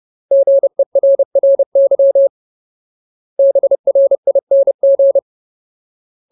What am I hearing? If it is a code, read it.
Morse code, heard as a keyed tone, about 30 wpm, 550 Hz. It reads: GERRY BRING
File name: Ultimate Word List / 30WPM Ultimate